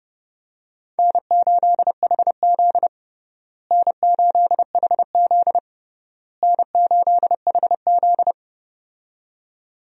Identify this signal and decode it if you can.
Morse code, heard as a keyed tone, about 30 wpm, 700 Hz. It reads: N8HZ N8HZ N8HZ
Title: USA Call Signs - 4 Characters - Repeated 3x - 30wpm